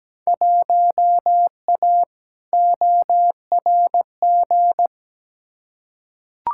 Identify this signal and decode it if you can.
Morse code, heard as a keyed tone, about 17 wpm, 700 Hz. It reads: 1A ORG
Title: Field Day Exchanges - 17wpm